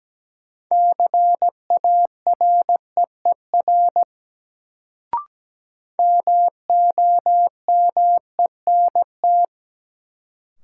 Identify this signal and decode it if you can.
Morse code, heard as a keyed tone, about 17 wpm, 700 Hz. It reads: CAREER MOMENT
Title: Sentences from Top 1000 Words - New Word Drill - 17wpm